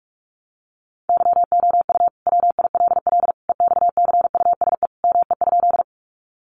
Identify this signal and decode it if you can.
Morse code, heard as a keyed tone, about 44 wpm, 700 Hz. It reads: YOU WILL EXCUSE ME?
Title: Sentences from Top 700 Words 45wpm